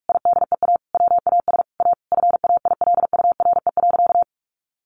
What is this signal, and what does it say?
Morse code, heard as a keyed tone, about 44 wpm, 700 Hz. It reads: IDEA WAS A FAILURE.
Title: Sentences from Top 900 Words 45wpm